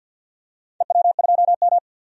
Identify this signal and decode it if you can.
Morse code, heard as a keyed tone, about 53 wpm, 700 Hz. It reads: EW2M
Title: DXCC-100 Call Signs - 4 Characters - 50wpm